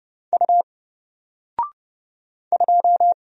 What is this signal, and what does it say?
Morse code, heard as a keyed tone, about 30 wpm, 700 Hz. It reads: U 2
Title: Single Character (Letter:Number:Punctuation:Prosign) 30wpm